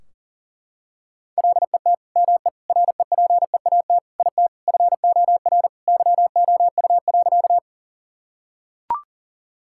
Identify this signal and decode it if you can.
Morse code, heard as a keyed tone, about 40 wpm, 700 Hz. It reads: LET ME REPEAT IT FOR YOU.
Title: Sentences from Top 1600 Words 40wpm